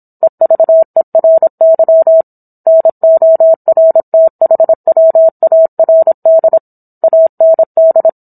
Morse code, heard as a keyed tone, about 27 wpm, 650 Hz. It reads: EVERY NORTHWARD AND